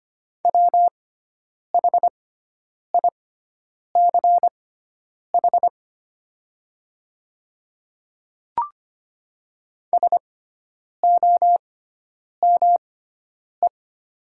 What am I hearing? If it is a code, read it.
Morse code, heard as a keyed tone, about 25 wpm, spaced as for 9 wpm, 700 Hz. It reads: WHICH SOME
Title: 6x Character Spacing - Top 100 Words - 25wpm